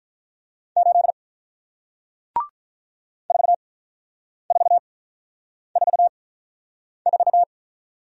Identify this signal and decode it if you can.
Morse code, heard as a keyed tone, about 46 wpm, spaced as for 9 wpm, 700 Hz. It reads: 8 4444